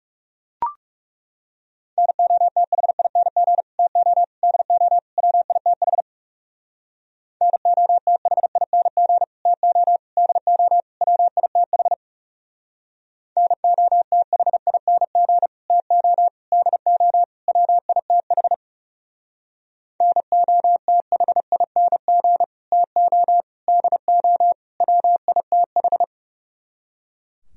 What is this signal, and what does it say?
Morse code, heard as a keyed tone, about 36 wpm, 700 Hz. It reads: NOTHING TO DO WITH NOTHING TO DO WITH NOTHING TO DO WITH NOTHING TO DO WITH
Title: Speed-Racing - Sets of 4 Words 30wpm